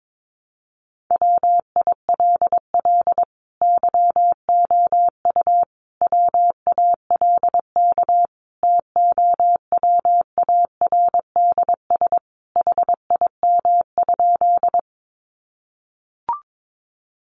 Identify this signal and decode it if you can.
Morse code, heard as a keyed tone, about 22 wpm, 700 Hz. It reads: WILL YOU WALK TOWARDS HIM?
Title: Sentences from Top 1100 Words 22wpm